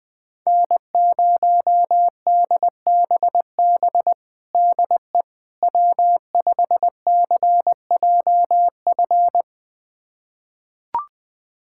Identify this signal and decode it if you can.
Morse code, heard as a keyed tone, about 21 wpm, 700 Hz. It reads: N0DBB DE W5CJF
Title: USA Callsign DE Callsign - 5 Characters - 20wpm